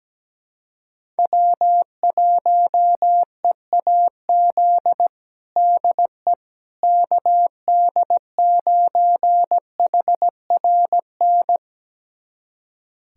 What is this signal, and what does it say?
Morse code, heard as a keyed tone, about 17 wpm, 700 Hz. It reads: W1EAZ DE KD9HRN
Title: USA Callsign DE Callsign 17wpm